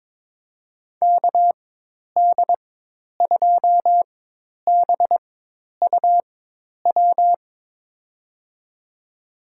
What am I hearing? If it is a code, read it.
Morse code, heard as a keyed tone, about 22 wpm, spaced as for 10 wpm, 700 Hz. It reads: KD2BUW
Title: USA Call Signs - 6 Characters - 4x Character Spacing - 22wpm